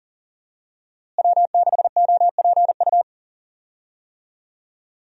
Morse code, heard as a keyed tone, about 40 wpm, 700 Hz. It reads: W6OPU
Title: USA Call Signs - 5 Characters - Repeated 3x - 40wpm